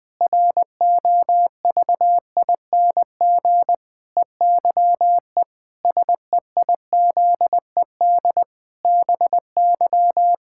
Morse code, heard as a keyed tone, about 19 wpm, 700 Hz. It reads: ROVING EYE SEIZED BY
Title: Pangram Challenge v2 - 20wpm